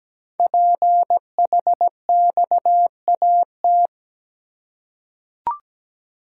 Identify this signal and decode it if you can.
Morse code, heard as a keyed tone, about 17 wpm, 700 Hz. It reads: PHXAT
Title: Code Groups - 5 Letter:Numbers - 17wpm